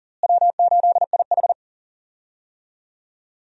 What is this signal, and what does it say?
Morse code, heard as a keyed tone, about 40 wpm, 700 Hz. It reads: W8IH
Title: USA Call Signs - 4 Characters - Repeated 2x - 40wpm